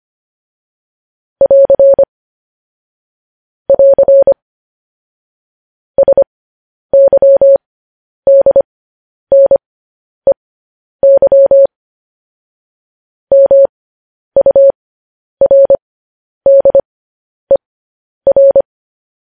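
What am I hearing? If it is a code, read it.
Morse code, heard as a keyed tone, about 25 wpm, spaced as for 10 wpm, 550 Hz. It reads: <AR> <AR> SYDNEY MURDER
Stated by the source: Morse code news headlines at 10.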